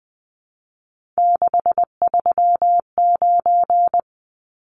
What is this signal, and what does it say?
Morse code, heard as a keyed tone, about 20 wpm, 700 Hz. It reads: 639
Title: Numbers - 3-Digit 20wpm